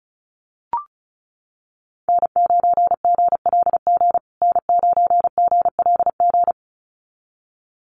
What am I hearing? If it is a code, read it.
Morse code, heard as a keyed tone, about 34 wpm, 700 Hz. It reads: N9GLG N9GLG
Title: USA Call Signs Repeated 2x 35wpm